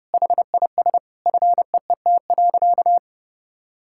Morse code, heard as a keyed tone, about 32 wpm, 700 Hz. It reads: HIS FEET.
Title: Sentences from Top 1000 Words - Review Part 5 - 30wpm